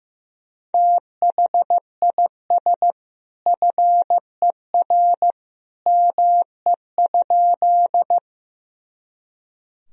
Morse code, heard as a keyed tone, about 15 wpm, 700 Hz. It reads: THIS FER ME?
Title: Sentences from Top 200 Words and Abbreviations 15wpm